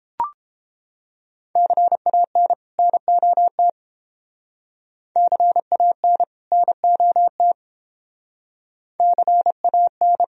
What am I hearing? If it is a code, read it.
Morse code, heard as a keyed tone, about 29 wpm, 700 Hz. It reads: CAN NOT CAN NOT CAN